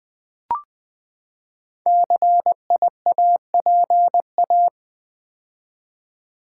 Morse code, heard as a keyed tone, about 20 wpm, 700 Hz. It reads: CIAPA
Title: Code Groups - 5 Letters - 20wpm